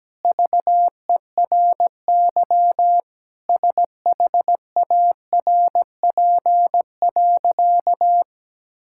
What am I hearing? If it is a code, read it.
Morse code, heard as a keyed tone, about 17 wpm, 700 Hz. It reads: VERY SHARP.
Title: Sentences from Top 1600 Words 17wpm